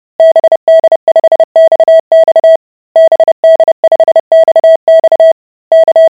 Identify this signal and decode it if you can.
Morse code, heard as a keyed tone, about 30 wpm, 650 Hz. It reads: BD5XX BD5XX K